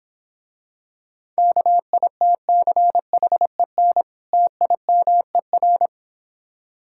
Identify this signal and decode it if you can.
Morse code, heard as a keyed tone, about 25 wpm, 700 Hz. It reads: KITCHEN TIMER